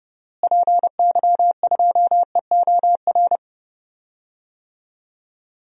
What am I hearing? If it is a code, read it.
Morse code, heard as a keyed tone, about 30 wpm, 700 Hz. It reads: PY2EOR
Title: DXCC-100 Call Signs - 6 Characters - Repeated 2x - 30wpm